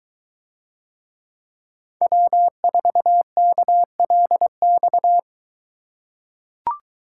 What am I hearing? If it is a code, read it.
Morse code, heard as a keyed tone, about 23 wpm, 700 Hz. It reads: W4KLX